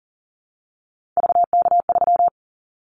Morse code, heard as a keyed tone, about 38 wpm, 700 Hz. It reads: VK3
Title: Callsign Prefixes for Top 100 DXCC - 40wpm